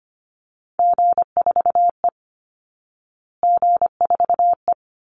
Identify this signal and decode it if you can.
Morse code, heard as a keyed tone, about 25 wpm, 700 Hz. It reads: G4E G4E
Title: Call Signs Repeated 2x - 25wpm